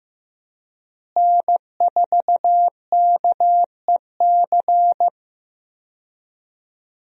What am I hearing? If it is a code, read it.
Morse code, heard as a keyed tone, about 15 wpm, 700 Hz. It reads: N4KEC
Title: USA Call Signs - 5 Characters - Repeated 3x - 15wpm